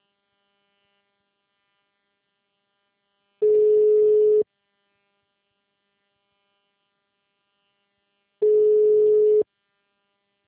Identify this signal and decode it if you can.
Ringback tone (Europe, 425 Hz).